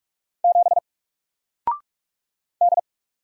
Morse code, heard as a keyed tone, about 44 wpm, 700 Hz. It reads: 7 D
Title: Single Character (Letter:Number:Punctuation:Prosign) 45wpm